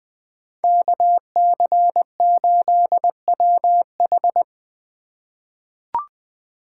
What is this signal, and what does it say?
Morse code, heard as a keyed tone, about 20 wpm, 700 Hz. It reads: KC8WH
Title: K1USN SST QSO Practice - 20wpm